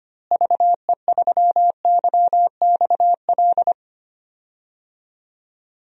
Morse code, heard as a keyed tone, about 25 wpm, 700 Hz. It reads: VE3YXL
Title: DXCC-100 Call Signs - 6 Characters - Repeated 2x - 25wpm